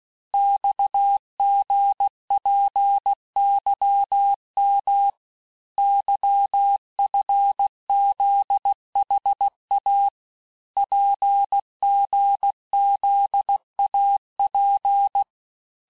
Morse code, heard as a keyed tone, about 16 wpm, 800 Hz. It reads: XGPYM YFZHA PGZAP
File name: cours-du-lundi files / Cours-1586-texte-chiffre-vitesse-15mpm